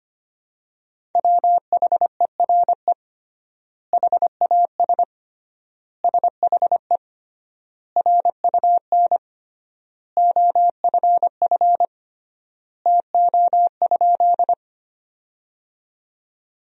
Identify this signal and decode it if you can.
Morse code, heard as a keyed tone, about 25 wpm, 700 Hz. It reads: WHERE HAS SHE RUN OFF TO?
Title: 3x Word-Spacing - Sentences from Top 300 Words - 25wpm